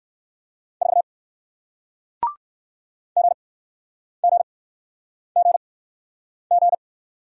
Morse code, heard as a keyed tone, about 60 wpm, spaced as for 9 wpm, 700 Hz. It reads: 4 GGGG